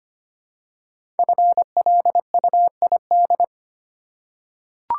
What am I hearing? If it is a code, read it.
Morse code, heard as a keyed tone, about 25 wpm, 700 Hz. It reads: FLUID
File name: 5 Letter Words - First 3 Letters Given - 25wpm